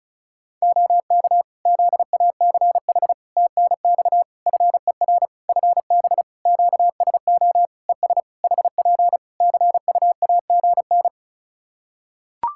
Morse code, heard as a keyed tone, about 34 wpm, 700 Hz. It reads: OK ZACH TNX FER FB QSO ES HP CUAGN
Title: K1USN SST QSO Practice 35wpm